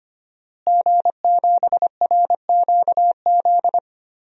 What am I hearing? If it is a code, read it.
Morse code, heard as a keyed tone, about 25 wpm, 700 Hz. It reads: G7RQZ
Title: DXCC-100 Call Signs - 5 Characters - Repeated 2x - 25wpm